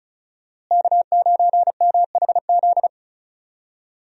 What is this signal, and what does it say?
Morse code, heard as a keyed tone, about 34 wpm, 700 Hz. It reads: K9MHZ
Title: USA Call Signs - 5 Characters - Repeated 2x - 35wpm